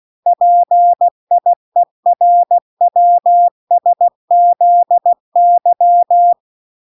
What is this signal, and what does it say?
Morse code, heard as a keyed tone, about 16 wpm, 700 Hz. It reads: PIERWSZY